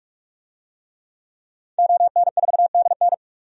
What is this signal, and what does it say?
Morse code, heard as a keyed tone, about 47 wpm, 700 Hz. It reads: ON4DN
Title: DXCC-100 Call Signs - 5 Characters - 45wpm